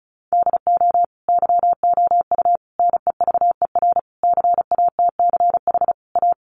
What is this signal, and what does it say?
Morse code, heard as a keyed tone, about 34 wpm, 700 Hz. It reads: DO YOU NEVER CATCH A
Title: Sentences from Top 1500 Words 35wpm